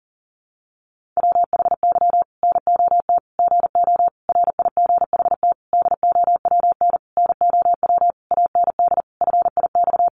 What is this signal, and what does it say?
Morse code, heard as a keyed tone, about 40 wpm, 700 Hz. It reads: WHY NOT GO RIGHT DOWN NOW AND FIX
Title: Sentences from Top 1400 Words 40wpm